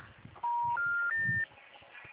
Special information tone.